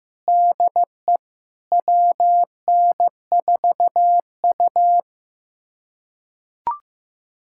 Morse code, heard as a keyed tone, about 15 wpm, 700 Hz. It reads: DE WN4U
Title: USA Callsign DE Callsign - 4 Characters - 15wpm